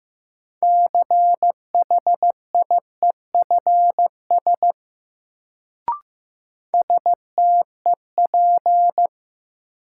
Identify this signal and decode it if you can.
Morse code, heard as a keyed tone, about 15 wpm, 700 Hz. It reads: CHIEFS STEP
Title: Sets of 5 Words Encore - New Word Drill - 15wpm